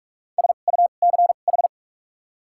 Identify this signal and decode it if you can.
Morse code, heard as a keyed tone, about 46 wpm, spaced as for 32 wpm, 700 Hz. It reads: SUCH
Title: 2x Character Spacing - Top 200 Words - 45wpm